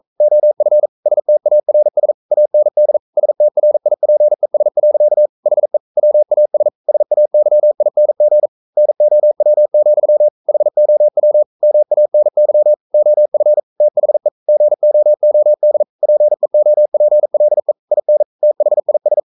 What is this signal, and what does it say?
Morse code, heard as a keyed tone, about 42 wpm, 600 Hz. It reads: OF STARS AND STRIPES. HE WAS SAYING NOW, HOW MANY OF THE GOOD PEOPLE IN THIS
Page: Romanen Never af Ken Follet på engelsk.